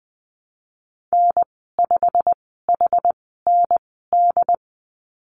Morse code, heard as a keyed tone, about 20 wpm, spaced as for 15 wpm, 700 Hz. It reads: N5HND
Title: 088 Lesson 15g - Number 5 - Call Signs 20wpm